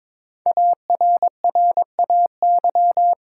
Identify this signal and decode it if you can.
Morse code, heard as a keyed tone, about 22 wpm, 700 Hz. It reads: ARRAY
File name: 5 Letter Words - First 3 Letters Given - 22wpm